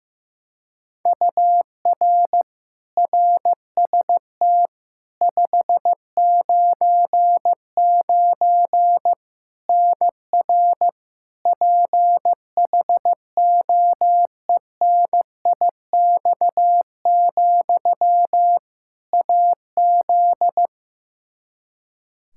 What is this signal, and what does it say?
Morse code, heard as a keyed tone, about 16 wpm, 700 Hz. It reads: UR RST 599 NR PHOENIX, AZ
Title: Practice QSO - Exchange 1 - Beginning - 15wpm